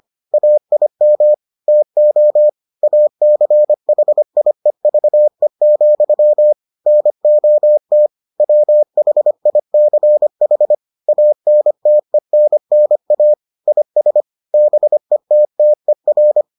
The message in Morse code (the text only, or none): AIM TO ACHIEVE, NOT WHICH ANTENNA IS BETTER